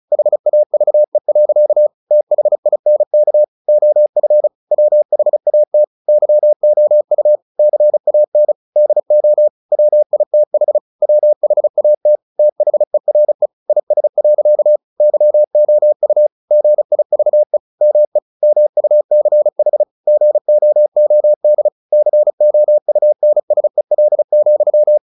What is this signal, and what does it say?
Morse code, heard as a keyed tone, about 34 wpm, 600 Hz. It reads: HAVE. THINK OF WHAT YOU CAN DO WITH WHAT THERE IS. YOU GIVE ME MUCH GOOD COUNSEL,
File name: the old man and the sea 2 35wpm